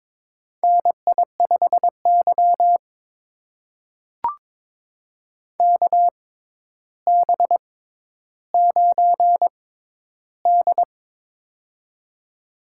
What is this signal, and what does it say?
Morse code, heard as a keyed tone, about 23 wpm, spaced as for 11 wpm, 700 Hz. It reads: NI5Y KB9D
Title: USA Call Signs - 4 Characters - 6x Character Spacing - 22wpm